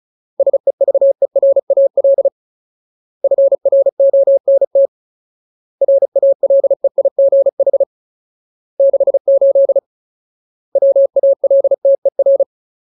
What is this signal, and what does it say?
Morse code, heard as a keyed tone, about 34 wpm, 550 Hz. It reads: SEVERAL FRONT RALEIGH 68 WALTER
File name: Ultimate Word List / 35WPM Ultimate